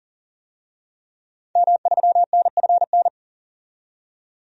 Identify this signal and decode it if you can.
Morse code, heard as a keyed tone, about 40 wpm, 700 Hz. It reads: M3NFN
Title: DXCC-100 Call Signs - 5 Characters - 40wpm